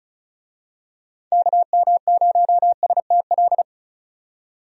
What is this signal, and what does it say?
Morse code, heard as a keyed tone, about 34 wpm, 700 Hz. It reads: KM0STL
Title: USA Call Signs - 6 Characters - 35wpm